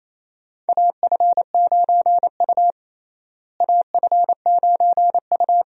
Morse code, heard as a keyed tone, about 28 wpm, 700 Hz. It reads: AF9U AF9U
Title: USA Call Signs - 4 Characters - Repeated 2x - 28wpm